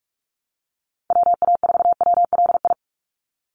Morse code, heard as a keyed tone, about 46 wpm, 700 Hz. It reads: WA4WLI
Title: Call Signs - 45wpm